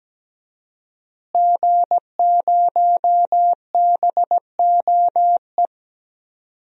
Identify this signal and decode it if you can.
Morse code, heard as a keyed tone, about 17 wpm, 700 Hz. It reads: G0BOE
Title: DXCC-100 Call Signs - 5 Characters - Repeated 2x - 17wpm